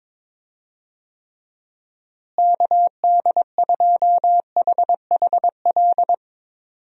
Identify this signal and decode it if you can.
Morse code, heard as a keyed tone, about 22 wpm, 700 Hz. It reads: KD2HHL
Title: USA Call Signs - 6 Characters - Repeated 3x - 22wpm